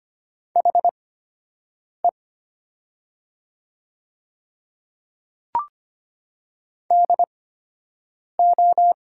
Morse code, heard as a keyed tone, about 25 wpm, spaced as for 7 wpm, 700 Hz. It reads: HE DO
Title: 8x Character Spacing - Top 100 Words - 25wpm